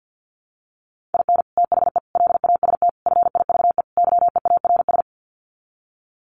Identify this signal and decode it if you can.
Morse code, heard as a keyed tone, about 49 wpm, 700 Hz. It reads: IN THE LAST FIVE YEARS
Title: Sets of 5 Words Encore 50wpm